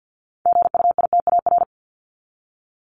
Morse code, heard as a keyed tone, about 50 wpm, 700 Hz. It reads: GUITAR
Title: Top 1501-1750 Words 50wpm